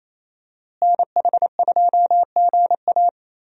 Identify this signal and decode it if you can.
Morse code, heard as a keyed tone, about 28 wpm, 700 Hz. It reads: NH2GA
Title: USA Call Signs - 5 Characters - Repeated 3x - 28wpm